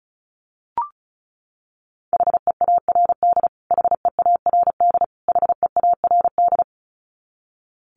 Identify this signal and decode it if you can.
Morse code, heard as a keyed tone, about 34 wpm, 700 Hz. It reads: HEARD HEARD HEARD
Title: Top 500 Words Repeated 35wpm